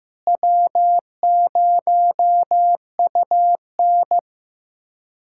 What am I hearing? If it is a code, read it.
Morse code, heard as a keyed tone, about 15 wpm, 700 Hz. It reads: W0UN
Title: USA Callsign DE Callsign - 4 Characters - 15wpm